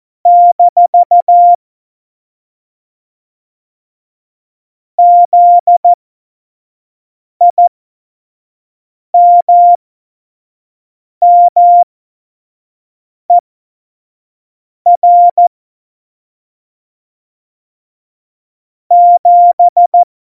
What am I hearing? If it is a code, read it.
Morse code, heard as a keyed tone, about 14 wpm, spaced as for 5 wpm, 700 Hz. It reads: - ZIMMER 7
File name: Morsekurs / Witze / Witz 13 March 2018, 25 F70